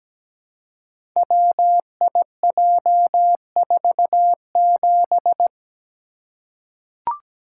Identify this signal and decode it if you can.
Morse code, heard as a keyed tone, about 17 wpm, 700 Hz. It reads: WIJ47
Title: Code Groups - 5 Letter:Numbers - 17wpm